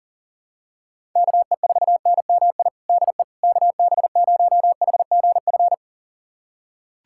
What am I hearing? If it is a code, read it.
Morse code, heard as a keyed tone, about 40 wpm, 700 Hz. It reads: KE4NMI DE KB0HGF
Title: USA Callsign DE Callsign - 6 Characters - 40wpm